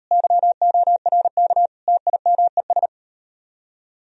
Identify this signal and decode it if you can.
Morse code, heard as a keyed tone, about 38 wpm, 700 Hz. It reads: YORK TIMES